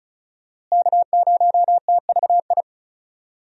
Morse code, heard as a keyed tone, about 34 wpm, 700 Hz. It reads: K0TVI
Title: USA Call Signs - 5 Characters - Repeated 3x - 35wpm